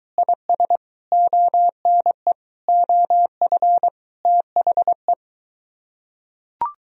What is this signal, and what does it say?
Morse code, heard as a keyed tone, about 24 wpm, 700 Hz. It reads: IS ONE OF THE